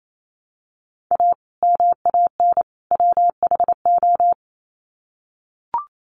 Morse code, heard as a keyed tone, about 28 wpm, 700 Hz. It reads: A MAN WHO
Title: Sets of 3 Words 28wpm